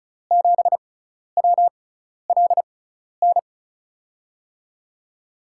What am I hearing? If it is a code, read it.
Morse code, heard as a keyed tone, about 36 wpm, spaced as for 12 wpm, 690 Hz. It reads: 7WLN
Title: USA Call Signs - 6 Characters - 6x Character Spacing - 35wpm